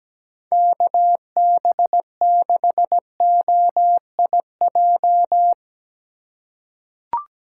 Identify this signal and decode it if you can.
Morse code, heard as a keyed tone, about 17 wpm, 700 Hz. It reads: KB6OIJ
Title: USA Call Signs - 6 Characters - 2x Character Spacing - 17wpm